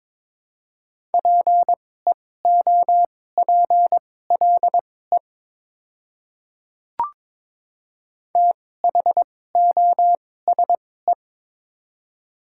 Morse code, heard as a keyed tone, about 22 wpm, spaced as for 16 wpm, 700 Hz. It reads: PEOPLE THOSE
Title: 2x Character Spacing - Top 300 Words - 22wpm